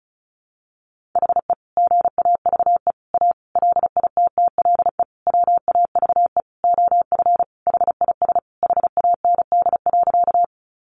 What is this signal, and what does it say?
Morse code, heard as a keyed tone, about 34 wpm, 700 Hz. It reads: HE GAVE A LITTLE WAVE OF HIS HAND.
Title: Sentences from Top 1400 Words 35wpm